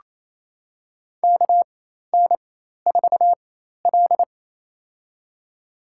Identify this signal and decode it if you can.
Morse code, heard as a keyed tone, about 27 wpm, spaced as for 13 wpm, 700 Hz. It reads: KN4L
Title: USA Call Signs - 4x Character Spacing - 28wpm